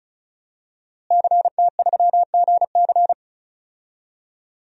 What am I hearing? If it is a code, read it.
Morse code, heard as a keyed tone, about 34 wpm, 700 Hz. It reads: CT3GC
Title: DXCC-100 Call Signs - 5 Characters - Repeated 3x - 35wpm